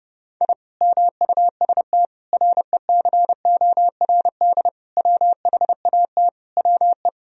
Morse code, heard as a keyed tone, about 30 wpm, 700 Hz. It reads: I MUST RECORD WHAT WE
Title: Sentences from Top 1000 Words - Review Part 3 - 30wpm